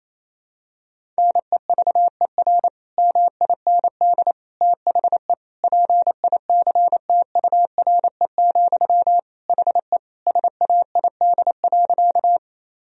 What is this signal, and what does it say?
Morse code, heard as a keyed tone, about 28 wpm, 700 Hz. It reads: NEVER MIND THE PICTURE, HE SAID.
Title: Sentences from Top 1000 Words - Review Part 1 - 28wpm